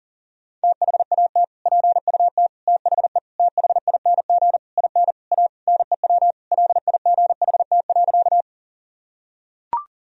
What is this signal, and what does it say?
Morse code, heard as a keyed tone, about 40 wpm, 700 Hz. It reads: THAT PUT THE THING IN A NEW LIGHT.
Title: Sentences from Top 1000 Words - Review Part 1 - 40wpm